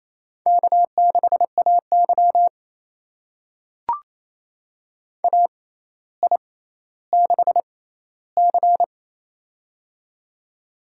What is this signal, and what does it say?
Morse code, heard as a keyed tone, about 28 wpm, spaced as for 14 wpm, 700 Hz. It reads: K6AY AI6C
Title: USA Call Signs - 6x Character Spacing - 28wpm